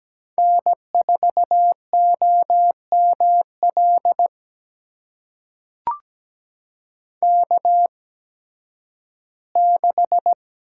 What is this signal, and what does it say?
Morse code, heard as a keyed tone, about 17 wpm, spaced as for 11 wpm, 700 Hz. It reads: N4OML K6
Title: USA Call Signs - 5 Characters - 8x Character Spacing - 17wpm